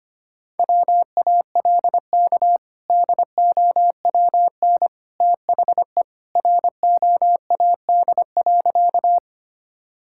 Morse code, heard as a keyed tone, about 25 wpm, 700 Hz. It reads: WALK DOWN THE ROAD.
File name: practice / 25wpm / Sentences with 2-Word Phrases - 25wpm